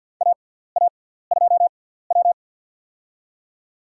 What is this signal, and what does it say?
Morse code, heard as a keyed tone, about 53 wpm, spaced as for 17 wpm, 700 Hz. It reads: AA2W
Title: USA Call Signs - 4 Characters - 6x Character Spacing - 50wpm